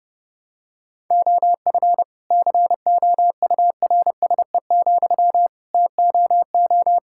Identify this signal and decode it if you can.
Morse code, heard as a keyed tone, about 30 wpm, 700 Hz. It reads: OF COURSE, TOO
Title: Sentences from Top 1000 Words - Review Part 5 - 30wpm